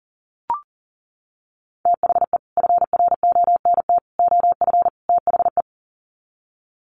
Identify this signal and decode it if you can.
Morse code, heard as a keyed tone, about 40 wpm, 700 Hz. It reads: THE FRONT OF THE
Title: Sets of 4 Words 40wpm